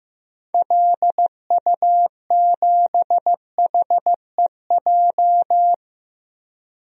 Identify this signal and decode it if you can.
Morse code, heard as a keyed tone, about 16 wpm, 700 Hz. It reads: LU7HEJ
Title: DXCC-100 Call Signs - 6 Characters - Repeated 3x - 15wpm